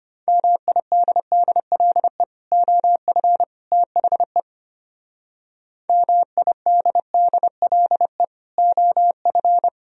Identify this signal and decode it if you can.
Morse code, heard as a keyed tone, about 32 wpm, 700 Hz. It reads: MIDDLE OF THE MIDDLE OF